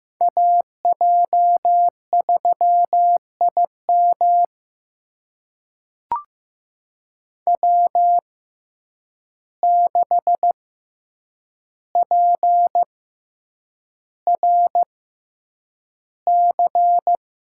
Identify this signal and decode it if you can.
Morse code, heard as a keyed tone, about 15 wpm, spaced as for 8 wpm, 700 Hz. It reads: AJ3IM W6PRC
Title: USA Call Signs - 5 Characters - 6x Character Spacing - 15wpm